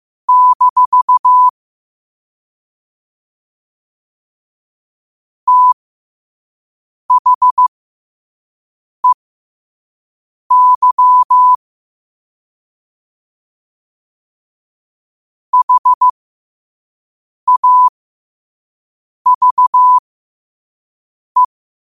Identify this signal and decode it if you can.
Morse code, heard as a keyed tone, about 15 wpm, spaced as for 5 wpm, 1000 Hz. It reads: - THEY HAVE
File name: qotdfiles / 5wpm